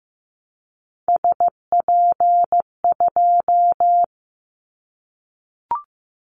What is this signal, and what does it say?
Morse code, heard as a keyed tone, about 15 wpm, 700 Hz. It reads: SP2
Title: Callsign Prefixes for Top 25 DXCC - Repeated 3x - 15wpm